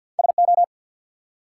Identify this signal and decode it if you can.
Morse code, heard as a keyed tone, about 50 wpm, 700 Hz. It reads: SO
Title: Two Letter Words - 50wpm